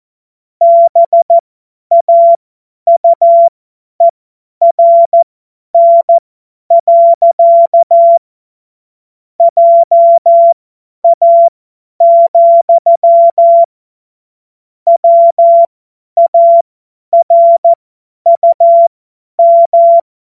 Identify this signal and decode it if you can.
Morse code, heard as a keyed tone, about 14 wpm, spaced as for 10 wpm, 680 Hz. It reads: BAUERN. JA, WARUM
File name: Morsekurs / Witze / Witz 04. April 2023 50 F70